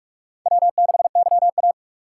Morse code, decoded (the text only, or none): W6YA